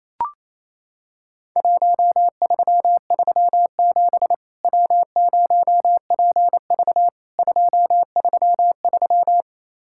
Morse code, heard as a keyed tone, about 27 wpm, 700 Hz. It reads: 1337 W0PV 233
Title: Worked All Europe DX Contest - QTC Exchange - 28wpm